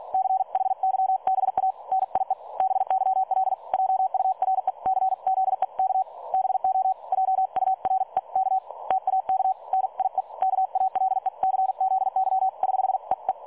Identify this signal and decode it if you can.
Morse code, heard as a keyed tone, about 48 wpm, 760 Hz. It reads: CHYBA NIE BYL JURECZEK BO JUREK TAK NIE KALECZY <HH> EE
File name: dlazaawansowanych / nagraniazpasma / websdr recording 3565.0kHz